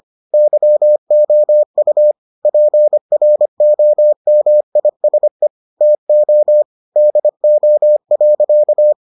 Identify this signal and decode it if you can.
Morse code, heard as a keyed tone, about 25 wpm, 600 Hz. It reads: YOU PROMISE TO DO.